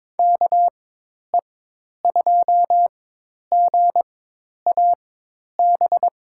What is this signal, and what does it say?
Morse code, heard as a keyed tone, about 22 wpm, spaced as for 10 wpm, 700 Hz. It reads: KE2GAB
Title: USA Call Signs - 4x Character Spacing - 22wpm